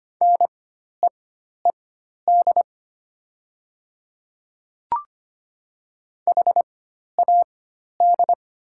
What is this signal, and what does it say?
Morse code, heard as a keyed tone, about 25 wpm, spaced as for 12 wpm, 700 Hz. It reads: NEED HAD
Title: 4x Character Spacing - Top 200 Words - 25wpm